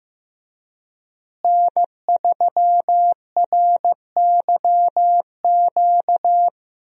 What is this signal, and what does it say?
Morse code, heard as a keyed tone, about 15 wpm, 700 Hz. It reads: N3RYQ
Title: USA Call Signs - 5 Characters - Repeated 2x - 15wpm